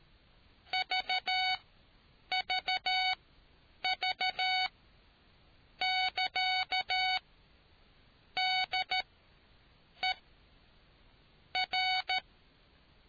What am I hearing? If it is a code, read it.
Morse code, heard as a keyed tone, about 13 wpm, spaced as for 6 wpm, 750 Hz. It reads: VVV<KA>DER